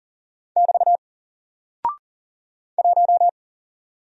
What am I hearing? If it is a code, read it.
Morse code, heard as a keyed tone, about 39 wpm, 700 Hz. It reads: <BT> 1
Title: Single Character (Letter:Number:Punctuation:Prosign) 40wpm